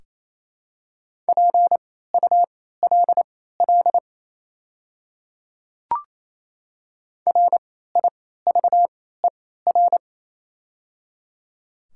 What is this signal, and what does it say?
Morse code, heard as a keyed tone, about 28 wpm, spaced as for 16 wpm, 700 Hz. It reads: PULL RIVER
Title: 3x Character Spacing - Top 500 Words - 28wpm